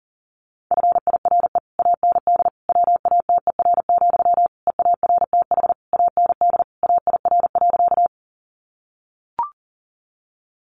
Morse code, heard as a keyed tone, about 40 wpm, 700 Hz. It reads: FIRE AND WATER, EARTH AND AIR.
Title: Sentences from Top 400 Words 40wpm